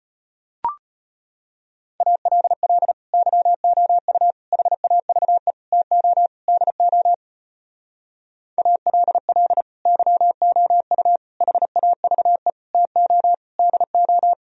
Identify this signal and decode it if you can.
Morse code, heard as a keyed tone, about 33 wpm, 700 Hz. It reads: ALL YOU HAVE TO DO ALL YOU HAVE TO DO